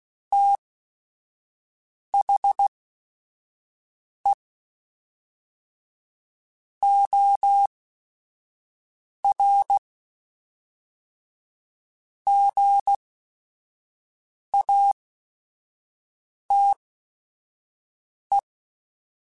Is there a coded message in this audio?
Yes: Morse code from 0.3 to 18.4 s, heard as a keyed tone, about 16 wpm, spaced as for 5 wpm, 800 Hz. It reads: THE OR GATE